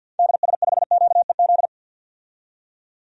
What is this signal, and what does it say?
Morse code, heard as a keyed tone, about 49 wpm, 700 Hz. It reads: DS5QEZ
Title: DXCC-100 Call Signs - 6 Characters - 50wpm